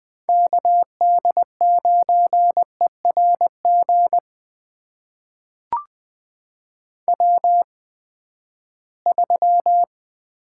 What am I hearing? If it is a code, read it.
Morse code, heard as a keyed tone, about 20 wpm, spaced as for 14 wpm, 700 Hz. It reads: KD9ERG W3
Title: USA Call Signs - 8x Character Spacing - 20wpm